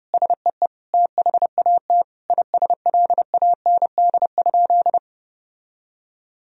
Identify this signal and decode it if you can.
Morse code, heard as a keyed tone, about 32 wpm, 700 Hz. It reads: SEE THAT ISLAND?
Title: Sentences from Top 1000 Words - Review Part 3 - 30wpm